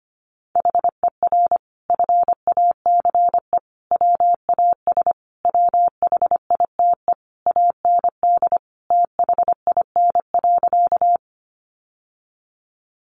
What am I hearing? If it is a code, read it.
Morse code, heard as a keyed tone, about 25 wpm, 700 Hz. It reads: HER FACE WAS WHITE AND THIN.
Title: Sentences from Top 1200 Words 25wpm